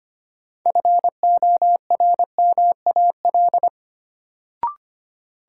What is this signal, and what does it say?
Morse code, heard as a keyed tone, about 25 wpm, 700 Hz. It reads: FORMAL
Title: Sentences from Top 1300 Words - New Word Drill - 25wpm